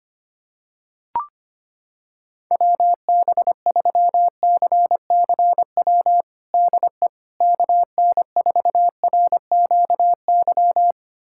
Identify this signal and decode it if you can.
Morse code, heard as a keyed tone, about 25 wpm, 700 Hz. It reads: WB3CCW DE KN4RQY
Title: USA Callsign DE Callsign - 6 Characters - 25wpm